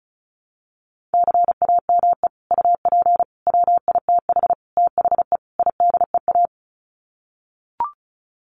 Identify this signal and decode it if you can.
Morse code, heard as a keyed tone, about 34 wpm, 700 Hz. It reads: CAME UP WITH THE IDEA
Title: Sets of 5 Words Encore2 35wpm